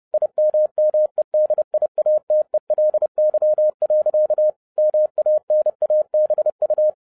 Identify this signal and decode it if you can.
Morse code, heard as a keyed tone, about 30 wpm, 610 Hz. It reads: IMMEDIATELY. MANABU
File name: 30wpm